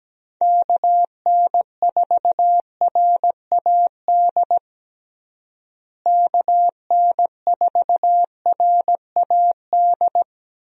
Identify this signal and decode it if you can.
Morse code, heard as a keyed tone, about 17 wpm, 700 Hz. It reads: KN4RAD KN4RAD
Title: USA Call Signs - 6 Characters - Repeated 2x - 17wpm